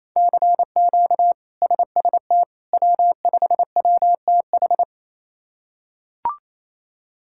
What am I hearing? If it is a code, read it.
Morse code, heard as a keyed tone, about 28 wpm, 700 Hz. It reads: CQ SST W5WTH
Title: K1USN SST QSO Practice - 28wpm